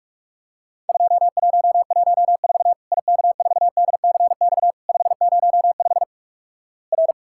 Morse code, heard as a keyed tone, about 47 wpm, 700 Hz. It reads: 2114 IK4DCX 505 R
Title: Worked All Europe DX Contest - QTC Exchange - 45wpm